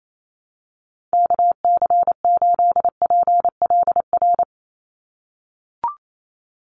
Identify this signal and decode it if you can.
Morse code, heard as a keyed tone, about 28 wpm, 700 Hz. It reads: KC8PLR
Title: USA Call Signs 28wpm